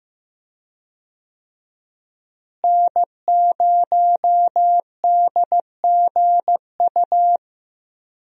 Morse code, heard as a keyed tone, about 15 wpm, 700 Hz. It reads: N0DGU
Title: USA Call Signs - 5 Characters - Repeated 3x - 15wpm